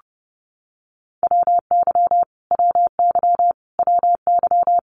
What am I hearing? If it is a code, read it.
Morse code, heard as a keyed tone, about 30 wpm, 700 Hz. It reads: WY WY WY
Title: US States Repeated 3x 30wpm